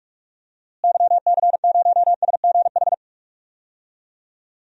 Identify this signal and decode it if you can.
Morse code, heard as a keyed tone, about 41 wpm, 700 Hz. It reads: YC0SGH
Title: DXCC-100 Call Signs - 6 Characters - Repeated 3x - 45wpm